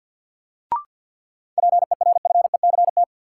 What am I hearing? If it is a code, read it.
Morse code, heard as a keyed tone, about 53 wpm, 700 Hz. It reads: PERFECT
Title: Sentences from Top 1000 Words - New Word Drill - 50wpm